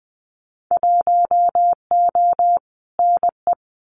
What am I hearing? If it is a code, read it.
Morse code, heard as a keyed tone, about 20 wpm, 700 Hz. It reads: 1O NE
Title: Winter Field Day Exchanges Encore 20wpm